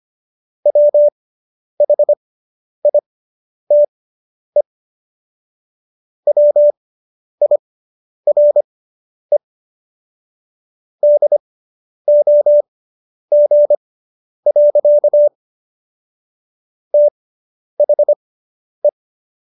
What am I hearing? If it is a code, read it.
Morse code, heard as a keyed tone, about 25 wpm, spaced as for 10 wpm, 590 Hz. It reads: WHITE WIRE DOG. THE